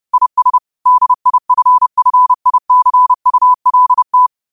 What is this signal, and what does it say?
Morse code, heard as a keyed tone, about 30 wpm, 1000 Hz. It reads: IS DIFFICULT
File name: 30wpm